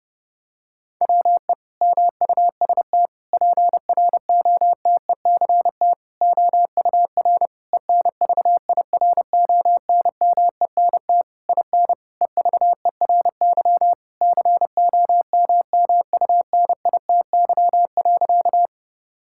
Morse code, heard as a keyed tone, about 32 wpm, 700 Hz. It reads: WE MUST PROTECT OUR ENVIRONMENT IN EVERY COMMUNITY.
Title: Sentences from Top 1000 Words - Review Part 5 - 30wpm